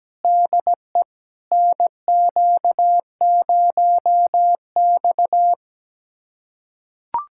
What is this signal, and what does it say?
Morse code, heard as a keyed tone, about 17 wpm, 700 Hz. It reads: DE NQ0X
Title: USA Callsign DE Callsign 17wpm